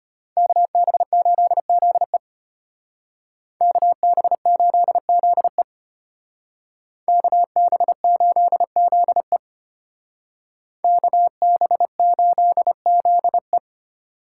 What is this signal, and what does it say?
Morse code, heard as a keyed tone, about 31 wpm, 700 Hz. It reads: KB8ZE KB8ZE KB8ZE KB8ZE
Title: Speed Racing - USA 2x2 Call Signs 25wpm